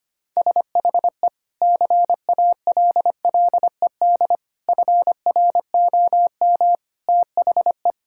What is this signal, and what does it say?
Morse code, heard as a keyed tone, about 25 wpm, 700 Hz. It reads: SHE CALLED FROM THE
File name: practice / 25wpm / Sentences with 2-Word Phrases - 25wpm